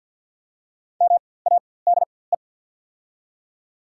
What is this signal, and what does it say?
Morse code, heard as a keyed tone, about 48 wpm, spaced as for 23 wpm, 700 Hz. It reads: MADE
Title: 4x Character Spacing - Top 300 Words - 50wpm